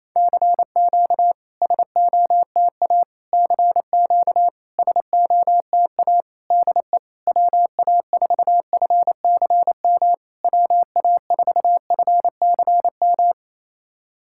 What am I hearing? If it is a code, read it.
Morse code, heard as a keyed tone, about 27 wpm, 690 Hz. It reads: CQ SOTA CQ SOTA DE WA4FCM WA4FCM
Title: SOTA QSO Practice 28wpm